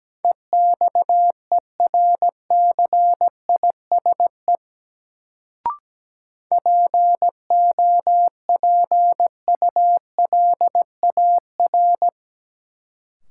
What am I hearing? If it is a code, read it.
Morse code, heard as a keyed tone, about 17 wpm, 700 Hz. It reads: EXERCISE POPULAR
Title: Sentences from Top 1200 Words - New Word Drill Alt - 17wpm